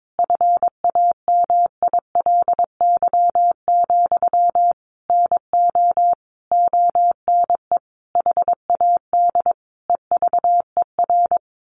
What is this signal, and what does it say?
Morse code, heard as a keyed tone, about 22 wpm, 700 Hz. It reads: FAMILY, NO ONE HAD EVER
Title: Sentences from Top 1200 Words 22wpm